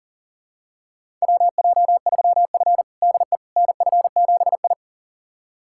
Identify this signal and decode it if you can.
Morse code, heard as a keyed tone, about 40 wpm, 700 Hz. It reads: WJ3F DE NF7I
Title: USA Callsign DE Callsign - 4 Characters - 40wpm